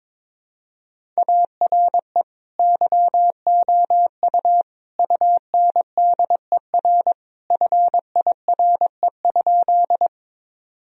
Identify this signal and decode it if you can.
Morse code, heard as a keyed tone, about 22 wpm, 700 Hz. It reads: ARE YOU UNDER FIRE?
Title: Sentences from Top 1000 Words - Review Part 1 - 22wpm